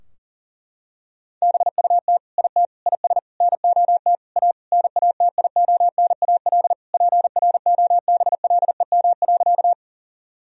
Morse code, heard as a keyed tone, about 40 wpm, 700 Hz. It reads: BUT IT IS NOT A NATIONAL PROBLEM.
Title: Sentences from Top 1000 Words - Review Part 4 - 40wpm